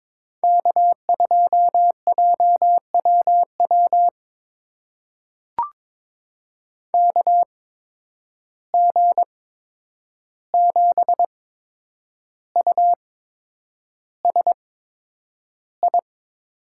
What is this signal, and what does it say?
Morse code, heard as a keyed tone, about 22 wpm, spaced as for 9 wpm, 700 Hz. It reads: K2JWW KG7USI
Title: USA Call Signs - 8x Character Spacing - 22wpm